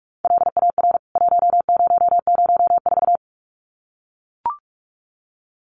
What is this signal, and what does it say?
Morse code, heard as a keyed tone, about 44 wpm, 700 Hz. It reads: LAR 1004
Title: CWT Exchange 45wpm